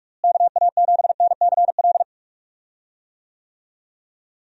Morse code, heard as a keyed tone, about 48 wpm, 700 Hz. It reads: KA7NCL
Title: USA Call Signs - 6 Characters - Repeated 3x - 45wpm